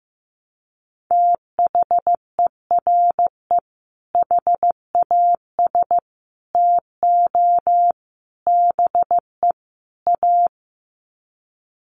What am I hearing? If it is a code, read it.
Morse code, heard as a keyed tone, about 15 wpm, 700 Hz. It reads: THERE HAS TO BE A
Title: Sets of 5 Words Encore 15wpm